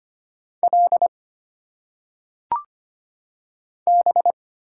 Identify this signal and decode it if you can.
Morse code, heard as a keyed tone, about 25 wpm, 700 Hz. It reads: L B
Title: Single Letters - Phonetic Pronunciation 25wpm